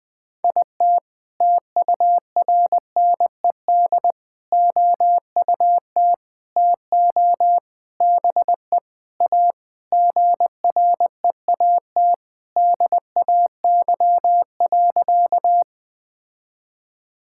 Morse code, heard as a keyed tone, about 20 wpm, 700 Hz. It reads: IT TURNED OUT TO BE A GREAT DAY.
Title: Sentences with 5-Word Phrases - 20wpm